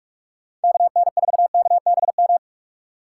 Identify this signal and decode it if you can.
Morse code, heard as a keyed tone, about 46 wpm, 700 Hz. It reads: KN4KBM
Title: USA Call Signs - 6x Character Spacing - 45wpm